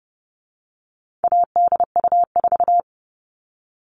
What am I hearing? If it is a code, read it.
Morse code, heard as a keyed tone, about 30 wpm, 700 Hz. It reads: ADU4
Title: All Cut Numbers - 4-Digit 30wpm